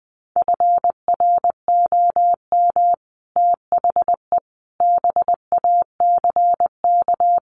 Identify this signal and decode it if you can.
Morse code, heard as a keyed tone, about 20 wpm, 700 Hz. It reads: FROM THE BACK
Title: Sets of 5 Words Encore 20wpm